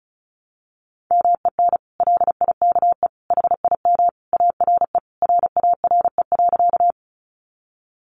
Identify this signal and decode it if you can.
Morse code, heard as a keyed tone, about 34 wpm, 700 Hz. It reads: MEN LIKE HIM ARE RARE.
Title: Sentences from Top 1100 Words 35wpm